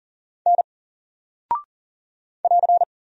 Morse code, heard as a keyed tone, about 39 wpm, 700 Hz. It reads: N <AR>
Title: Single Character (Letter:Number:Punctuation:Prosign) Alt 40wpm